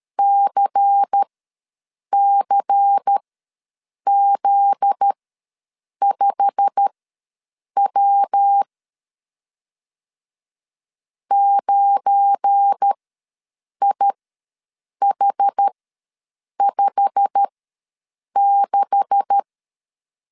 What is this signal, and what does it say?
Morse code, heard as a keyed tone, about 12 wpm, spaced as for 7 wpm, 800 Hz. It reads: CCZ5W 9IH56
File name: LEZIONI / 9 2867 ws 12wpm 3431